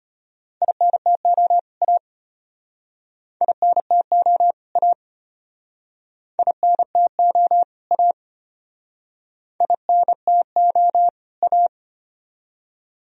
Morse code, heard as a keyed tone, about 33 wpm, 700 Hz. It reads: INTO A INTO A INTO A INTO A